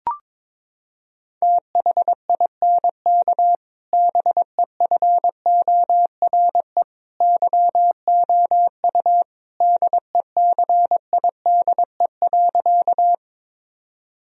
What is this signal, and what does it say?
Morse code, heard as a keyed tone, about 22 wpm, 700 Hz. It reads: THINK BEFORE YOU DECIDE.
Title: Sentences from Top 1000 Words - Review Part 3 - 22wpm